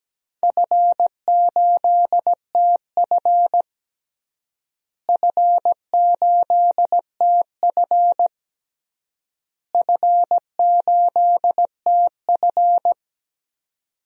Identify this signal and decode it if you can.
Morse code, heard as a keyed tone, about 17 wpm, 700 Hz. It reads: F8TF F8TF F8TF
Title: DXCC-100 Call Signs - 4 Characters - Repeated 3x - 17wpm